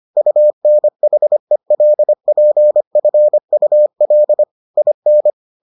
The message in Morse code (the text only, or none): UNHELPFUL IN